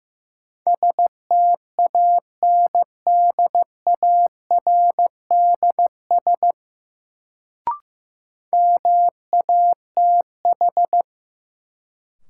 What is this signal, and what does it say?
Morse code, heard as a keyed tone, about 15 wpm, 700 Hz. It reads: STANDARDS MATH
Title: Sentences from Top 1000 Words - Review Part 6 - New Word Drill - 15wpm